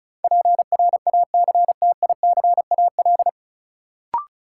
Morse code, heard as a keyed tone, about 34 wpm, 700 Hz. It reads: PRACTICAL
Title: Sentences from Top 1400 Words - New Word Drill Alt - 35wpm